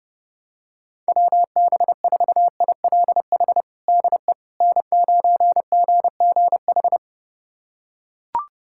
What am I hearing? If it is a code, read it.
Morse code, heard as a keyed tone, about 32 wpm, 700 Hz. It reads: WB4ILH DE N9GGH
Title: USA Callsign DE Callsign 30wpm